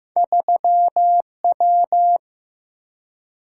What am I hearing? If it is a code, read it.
Morse code, heard as a keyed tone, about 15 wpm, 700 Hz. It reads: 3W
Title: Single-Letter Single-Number 15wpm